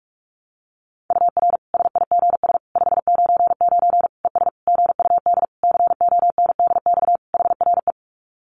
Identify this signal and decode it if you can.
Morse code, heard as a keyed tone, about 44 wpm, 700 Hz. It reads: UR SIGS 599 ES GUD CONDX HRE
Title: Common QSO Phrases 45wpm